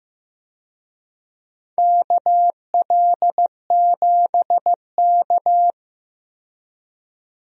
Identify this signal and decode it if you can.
Morse code, heard as a keyed tone, about 15 wpm, 700 Hz. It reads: KL7K
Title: DXCC-100 Call Signs - 4 Characters - 15wpm